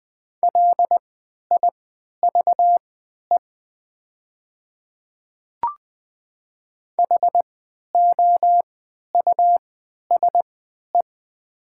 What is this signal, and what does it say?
Morse code, heard as a keyed tone, about 20 wpm, spaced as for 11 wpm, 700 Hz. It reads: LIVE HOUSE
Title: 3x Character Spacing - Top 300 Words - 20wpm